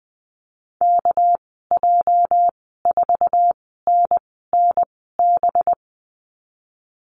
Morse code, heard as a keyed tone, about 20 wpm, spaced as for 14 wpm, 700 Hz. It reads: KJ4NNB
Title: 207 Lesson 33g - Letter j - Call Signs 20wpm